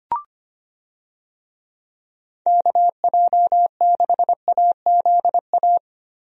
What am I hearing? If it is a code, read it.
Morse code, heard as a keyed tone, about 25 wpm, 700 Hz. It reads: KJ6AZA
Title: USA Call Signs - 6 Characters - Repeated 3x - 25wpm